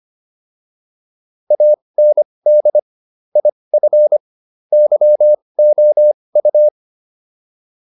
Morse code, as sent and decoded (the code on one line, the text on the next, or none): .- -. -.. / .. ..-. / -.-- --- ..-
AND IF YOU